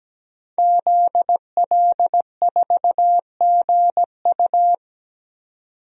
Morse code, heard as a keyed tone, about 17 wpm, 700 Hz. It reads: ZL4GU
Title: DXCC-100 Call Signs - 5 Characters - 17wpm